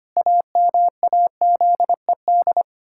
Morse code, heard as a keyed tone, about 25 wpm, 700 Hz. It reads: AMAZED
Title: Pangram Challenge v2 - 25wpm